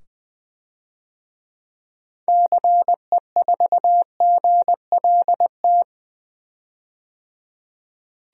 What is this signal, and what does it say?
Morse code, heard as a keyed tone, about 21 wpm, 700 Hz. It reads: CE4GLT
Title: DXCC-100 Call Signs - 6 Characters - 20wpm